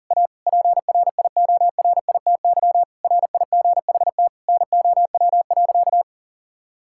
Morse code, heard as a keyed tone, about 40 wpm, 700 Hz. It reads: A PRIORITY RIGHT NOW.
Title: Sentences from Top 1600 Words 40wpm